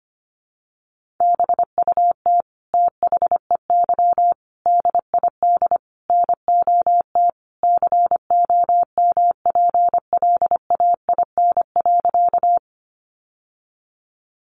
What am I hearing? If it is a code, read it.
Morse code, heard as a keyed tone, about 25 wpm, 700 Hz. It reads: BUT THEY DID NOT COMPLAIN.
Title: Sentences from Top 1300 Words 25wpm